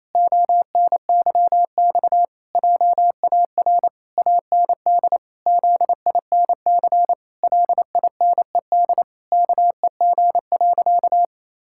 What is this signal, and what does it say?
Morse code, heard as a keyed tone, about 28 wpm, 700 Hz. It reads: ONYX JAR AND ZINC LINED KEG.
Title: Pangram Challenge v2 - 28wpm